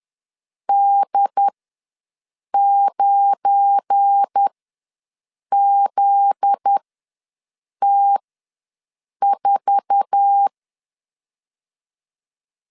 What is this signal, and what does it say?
Morse code, heard as a keyed tone, about 10 wpm, spaced as for 6 wpm, 800 Hz. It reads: D9ZT4
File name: LEZIONI / 10 SIGNS / 10D COMA ws 10wpm 4314